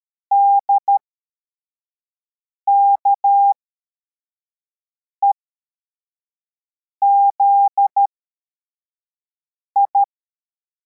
Morse code, heard as a keyed tone, about 13 wpm, spaced as for 4 wpm, 800 Hz. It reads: DKEZI
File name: formation / cw / Text44